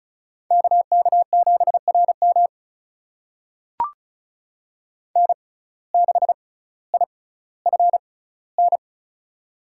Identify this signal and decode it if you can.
Morse code, heard as a keyed tone, about 36 wpm, spaced as for 18 wpm, 700 Hz. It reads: KK7RM N6IFN
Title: USA Call Signs - 5 Characters - 6x Character Spacing - 35wpm